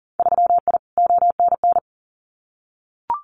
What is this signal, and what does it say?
Morse code, heard as a keyed tone, about 40 wpm, 700 Hz. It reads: 3I ONN
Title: Winter Field Day Exchanges Encore 40wpm